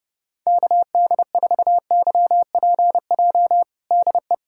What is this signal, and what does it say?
Morse code, heard as a keyed tone, about 32 wpm, 700 Hz. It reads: KD4YPJ DE
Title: USA Callsign DE Callsign - 6 Characters - 30wpm